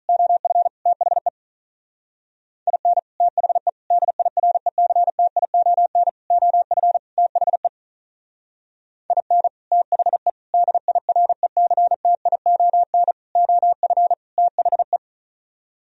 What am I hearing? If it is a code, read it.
Morse code, heard as a keyed tone, about 42 wpm, 700 Hz. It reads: OF THE IN THE DIRECTION OF THE IN THE DIRECTION OF THE
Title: Speed-Racing - Sets of 5 Words 35wpm